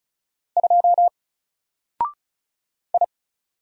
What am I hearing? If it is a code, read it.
Morse code, heard as a keyed tone, about 36 wpm, 700 Hz. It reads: 2 I
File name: Single-Letter-Number-35wpm